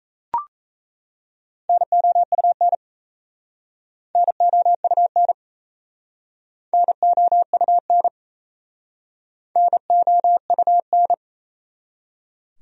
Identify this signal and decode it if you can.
Morse code, heard as a keyed tone, about 38 wpm, 700 Hz. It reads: NOUN NOUN NOUN NOUN